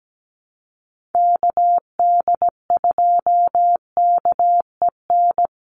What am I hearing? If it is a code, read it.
Morse code, heard as a keyed tone, about 17 wpm, 700 Hz. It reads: KD2KEN
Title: USA Call Signs 17wpm